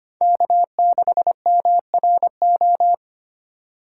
Morse code, heard as a keyed tone, about 25 wpm, 700 Hz. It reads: K6MRO
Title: USA Call Signs - 5 Characters - 3x Character Spacing - 25wpm